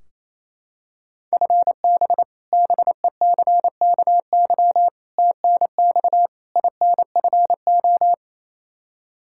Morse code, heard as a keyed tone, about 28 wpm, 700 Hz. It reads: FB BECKY TNX INFO
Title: Practice QSO - Exchange 3 - More Info - 28wpm